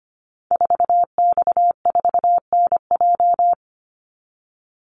Morse code, heard as a keyed tone, about 25 wpm, 700 Hz. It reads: 4X4NJ
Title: Call Signs - 25wpm